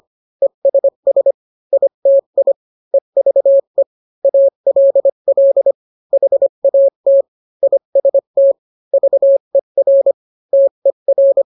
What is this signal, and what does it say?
Morse code, heard as a keyed tone, about 25 wpm, 550 Hz. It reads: ESS ITI EVE ALL HAT IST VER TER